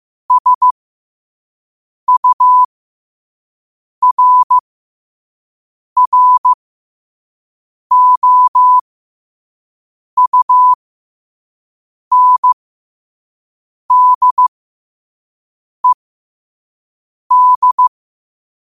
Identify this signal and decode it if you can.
Morse code, heard as a keyed tone, about 15 wpm, spaced as for 5 wpm, 1000 Hz. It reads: SURROUNDED
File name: qotdfiles / 5wpm